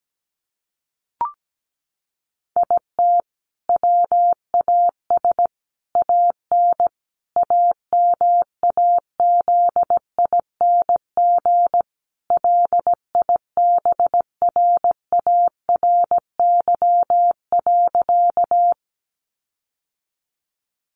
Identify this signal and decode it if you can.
Morse code, heard as a keyed tone, about 17 wpm, 700 Hz. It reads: IT WAS AN AMAZING LIBRARY.
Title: Sentences from Top 1500 Words 17wpm